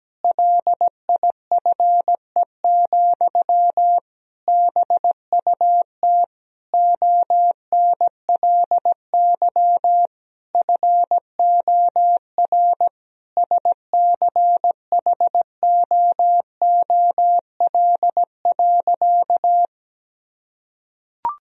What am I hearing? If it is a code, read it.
Morse code, heard as a keyed tone, about 17 wpm, 700 Hz. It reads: LIFE, BUT ONLY FOR SCHOOL.
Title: Sentences from Top 1000 Words - Review Part 1 - 17wpm